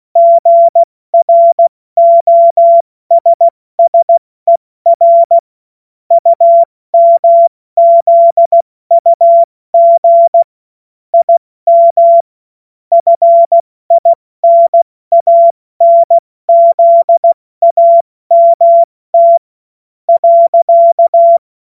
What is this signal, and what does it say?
Morse code, heard as a keyed tone, about 16 wpm, 680 Hz. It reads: GROSSER UMZUG IM FINANZAMT .
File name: Witz-07-Oktober-2025-70-F80